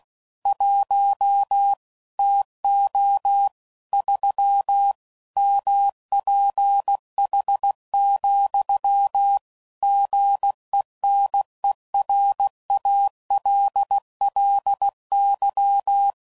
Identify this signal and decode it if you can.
Morse code, heard as a keyed tone, about 16 wpm, 800 Hz. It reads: 1 TO 3 MPH, GENERALLY
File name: CW Code Groups @15WPM/Weather Sets